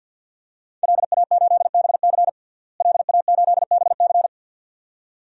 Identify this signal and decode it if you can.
Morse code, heard as a keyed tone, about 47 wpm, 700 Hz. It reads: LA8BC LA8BC
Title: DXCC-100 Call Signs - 5 Characters - Repeated 2x - 50wpm